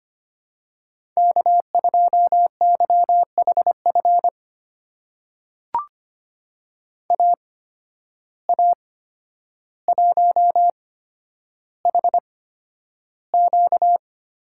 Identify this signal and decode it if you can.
Morse code, heard as a keyed tone, about 25 wpm, spaced as for 11 wpm, 700 Hz. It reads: K2YHF AA1HQ
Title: USA Call Signs - 5 Characters - 8x Character Spacing - 25wpm